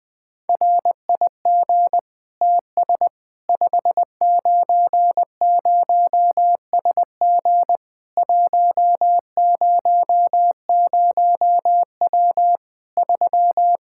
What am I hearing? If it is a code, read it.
Morse code, heard as a keyed tone, about 20 wpm, 700 Hz. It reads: RIG TS 590SG 100W 3
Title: K1USN SST QSO Practice 20wpm